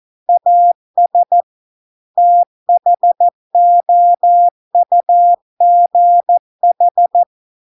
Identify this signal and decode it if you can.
Morse code, heard as a keyed tone, about 14 wpm, 700 Hz. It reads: AS THOUGH